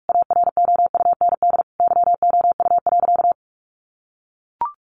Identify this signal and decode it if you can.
Morse code, heard as a keyed tone, about 44 wpm, 700 Hz. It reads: AROUND YOU.
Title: Sentences from Top 1200 Words 45wpm